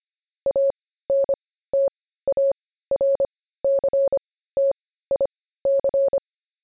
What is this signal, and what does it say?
Morse code, heard as a keyed tone, about 25 wpm, spaced as for 15 wpm, 550 Hz. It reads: ANTARCTIC
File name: jb recaptured / morse